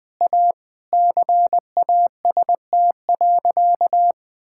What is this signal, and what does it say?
Morse code, heard as a keyed tone, about 20 wpm, 700 Hz. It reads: A CAST.
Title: All-Age Appropriate Jokes - 20wpm